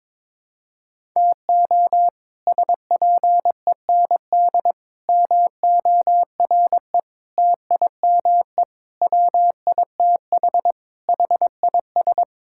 Morse code, heard as a keyed tone, about 22 wpm, 700 Hz. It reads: TO SPEND MORE TIME WITH HIS
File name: practice / 22wpm / Sets of 7 Words - 22wpm